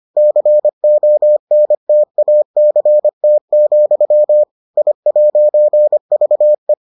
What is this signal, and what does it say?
Morse code, heard as a keyed tone, about 25 wpm, 600 Hz. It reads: CONTACT, I'VE